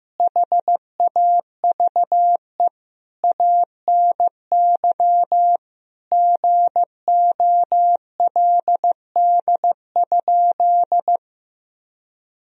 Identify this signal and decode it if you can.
Morse code, heard as a keyed tone, about 15 wpm, 700 Hz. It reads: HAVE ANY GOLD?
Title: Sentences from Top 1000 Words - Review Part 3 - 15wpm